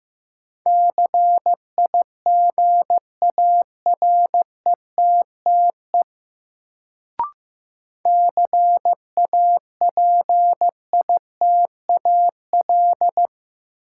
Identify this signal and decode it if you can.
Morse code, heard as a keyed tone, about 15 wpm, 700 Hz. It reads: CIGARETTE CAPITAL
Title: Sentences from Top 1000 Words - New Word Drill - 15wpm